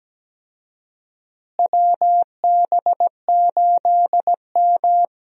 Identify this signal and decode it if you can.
Morse code, heard as a keyed tone, about 17 wpm, 700 Hz. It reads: WB8M
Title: USA Call Signs - 4 Characters - Repeated 3x - 17wpm